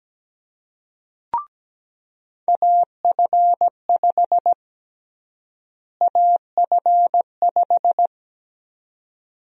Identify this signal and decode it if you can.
Morse code, heard as a keyed tone, about 17 wpm, 700 Hz. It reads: AF5 AF5
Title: USA Call Sign Prefixes - Repeated 2x - 17wpm